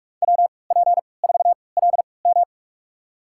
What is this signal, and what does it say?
Morse code, heard as a keyed tone, about 46 wpm, spaced as for 25 wpm, 700 Hz. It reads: WP4LM
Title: USA Call Signs - 3x Character Spacing - 45wpm